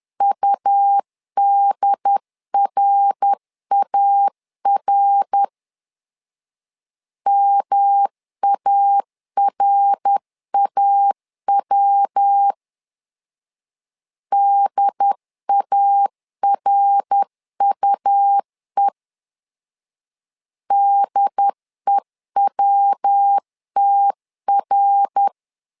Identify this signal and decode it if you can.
Morse code, heard as a keyed tone, about 10 wpm, 800 Hz. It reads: UDRAR MARAW DARUE DEWTR
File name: LEZIONI / 3 MWRO / 3 MWR 10wpm 1952